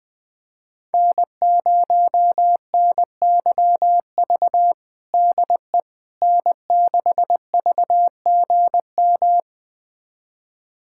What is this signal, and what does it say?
Morse code, heard as a keyed tone, about 21 wpm, 700 Hz. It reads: N0NYV DE N6VGM
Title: USA Callsign DE Callsign - 5 Characters - 20wpm